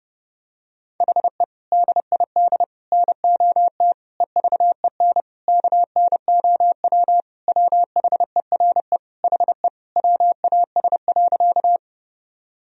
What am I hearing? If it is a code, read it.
Morse code, heard as a keyed tone, about 32 wpm, 700 Hz. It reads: HE DID NOT EVEN KNOW WHERE HE WAS.
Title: Sentences from Top 1000 Words - Review Part 1 - 30wpm